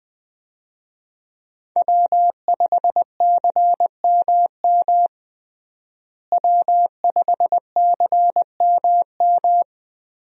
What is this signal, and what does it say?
Morse code, heard as a keyed tone, about 20 wpm, 700 Hz. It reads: W5CMM W5CMM
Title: USA Call Signs - 5 Characters - Repeated 2x - 20wpm